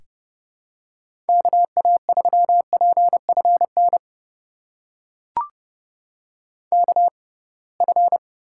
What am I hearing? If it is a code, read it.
Morse code, heard as a keyed tone, about 30 wpm, spaced as for 23 wpm, 700 Hz. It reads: KA3PFN KF
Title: USA Call Signs - 6 Characters - 6x Character Spacing - 30wpm